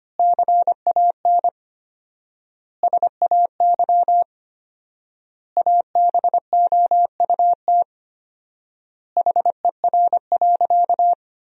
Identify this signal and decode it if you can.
Morse code, heard as a keyed tone, about 25 wpm, 700 Hz. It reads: CAN SAY ABOUT HER.
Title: 4x Word-Spacing - Sentences from Top 200 Words - 25wpm